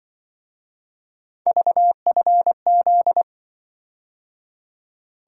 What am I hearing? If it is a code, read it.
Morse code, heard as a keyed tone, about 25 wpm, 700 Hz. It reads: VFZ
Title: Speed-Racing - Three Letter-Number Combinations - 20wpm